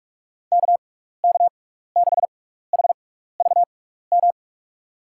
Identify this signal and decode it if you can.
Morse code, heard as a keyed tone, about 45 wpm, spaced as for 15 wpm, 700 Hz. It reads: KK6HVM
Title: USA Call Signs - 6x Character Spacing - 45wpm